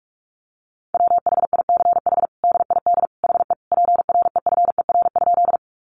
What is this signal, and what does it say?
Morse code, heard as a keyed tone, about 44 wpm, 700 Hz. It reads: WHICH DID HE PREFER?
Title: Sentences from Top 1100 Words 45wpm